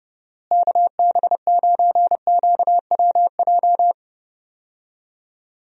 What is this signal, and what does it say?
Morse code, heard as a keyed tone, about 30 wpm, 700 Hz. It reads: KB9QWJ
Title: USA Call Signs - 6 Characters - Repeated 2x - 30wpm